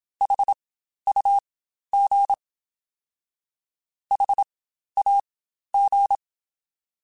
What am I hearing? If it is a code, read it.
Morse code, heard as a keyed tone, about 27 wpm, spaced as for 12 wpm, 800 Hz. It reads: HUG HAG
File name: Spaced Alphas